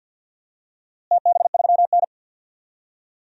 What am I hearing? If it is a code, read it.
Morse code, heard as a keyed tone, about 50 wpm, 700 Hz. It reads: TB3N
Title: DXCC-100 Call Signs - 4 Characters - Repeated 2x - 50wpm